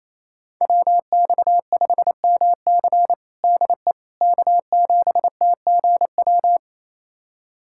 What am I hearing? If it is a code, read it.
Morse code, heard as a keyed tone, about 28 wpm, 700 Hz. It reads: WX5MC DE K7TGW
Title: USA Callsign DE Callsign - 5 Characters - 28wpm